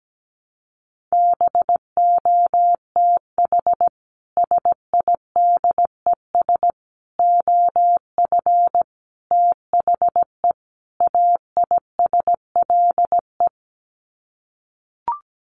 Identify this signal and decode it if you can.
Morse code, heard as a keyed tone, about 17 wpm, 700 Hz. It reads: BOTH SIDES OF THE AISLE
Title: Sets of 5 Words Encore2 17wpm